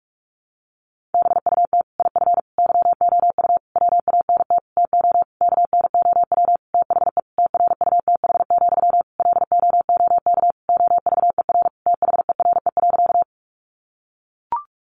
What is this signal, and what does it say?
Morse code, heard as a keyed tone, about 44 wpm, 700 Hz. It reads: BUT IF YOU WANT TO KNOW THE TRUTH, LOOK OVER THERE.
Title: Sentences from Top 800 Words 45wpm